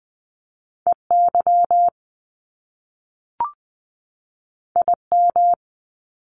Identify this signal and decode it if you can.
Morse code, heard as a keyed tone, about 20 wpm, 700 Hz. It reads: EY IM
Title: Top Two Letter Combinations in Words 20wpm